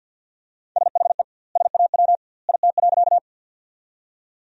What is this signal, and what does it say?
Morse code, heard as a keyed tone, about 50 wpm, 700 Hz. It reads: SHE SAW IT.
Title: 2x Word-Spacing - Sentences from Top 300 Words - 50wpm